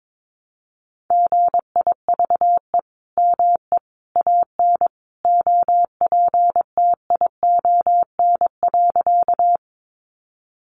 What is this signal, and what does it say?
Morse code, heard as a keyed tone, about 22 wpm, 700 Hz. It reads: GIVE ME AN OPTION.
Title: Sentences from Top 1300 Words 22wpm